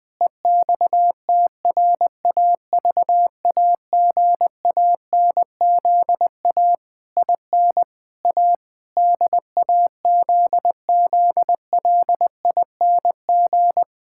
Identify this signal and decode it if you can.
Morse code, heard as a keyed tone, about 21 wpm, 700 Hz. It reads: EXTRAVAGANZA IN A DAZZLING
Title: Pangram Challenge v2 - 20wpm